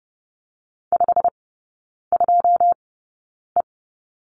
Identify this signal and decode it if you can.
Morse code, heard as a keyed tone, about 30 wpm, spaced as for 9 wpm, 700 Hz. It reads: 52E
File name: course / 30:10wpm / 087 Lesson 15e - Number 5 - Three Characters 30wpm